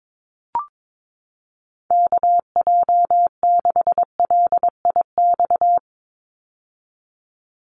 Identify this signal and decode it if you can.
Morse code, heard as a keyed tone, about 22 wpm, 700 Hz. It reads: KJ6LIX
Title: USA Call Signs 22wpm